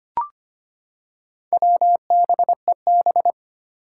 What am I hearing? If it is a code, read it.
Morse code, heard as a keyed tone, about 25 wpm, 700 Hz. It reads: WBEB
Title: Four Challenging Letter-Number Combinations 25wpm